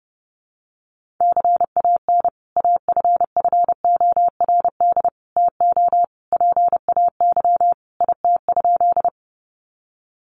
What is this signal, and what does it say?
Morse code, heard as a keyed tone, about 30 wpm, 700 Hz. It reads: CAN AFFORD TO PAY IT?
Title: Sentences from Top 1100 Words 30wpm